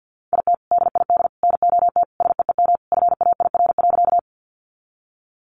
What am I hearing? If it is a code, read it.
Morse code, heard as a keyed tone, about 50 wpm, 700 Hz. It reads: IT DID NOT SEEM FAIR.
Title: Sentences from Top 1300 Words 50wpm